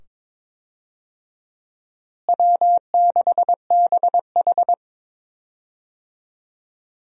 Morse code, heard as a keyed tone, about 22 wpm, 700 Hz. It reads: W6BH
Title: USA Call Signs - 4 Characters - Repeated 2x - 22wpm